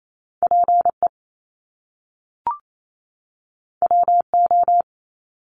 Morse code, heard as a keyed tone, about 28 wpm, 700 Hz. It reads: PE WO
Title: Top Two Letter Combinations in Words 28wpm